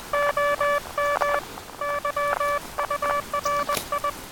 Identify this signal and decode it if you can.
Morse code, heard as a keyed tone, about 20 wpm, 1160 Hz, with noise. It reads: OM YURI
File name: attachments / 23.04.2017 DSB TRX OM3JA